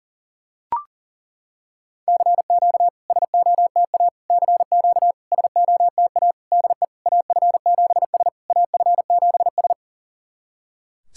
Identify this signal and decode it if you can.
Morse code, heard as a keyed tone, about 40 wpm, 690 Hz. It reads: CQ SOTA CQ SOTA DE AF7S AF7S
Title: SOTA QSO Practice 40wpm